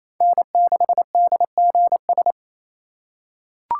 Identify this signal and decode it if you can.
Morse code, heard as a keyed tone, about 27 wpm, 700 Hz. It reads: N6DGS
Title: USA Call Signs - 4x Character Spacing - 28wpm